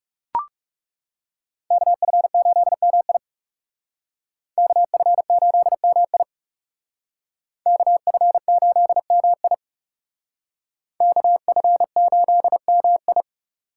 Morse code, heard as a keyed tone, about 36 wpm, 700 Hz. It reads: KF8MI KF8MI KF8MI KF8MI
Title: Speed Racing - USA Call Signs 30wpm